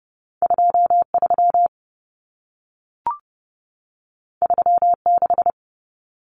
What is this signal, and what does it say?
Morse code, heard as a keyed tone, about 30 wpm, 700 Hz. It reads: 23 36
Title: Numbers - 2-Digit 30wpm